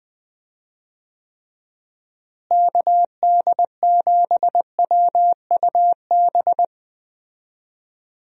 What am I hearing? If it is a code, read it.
Morse code, heard as a keyed tone, about 20 wpm, 700 Hz. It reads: KD7WUB
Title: USA Call Signs - 6 Characters - Repeated 3x - 20wpm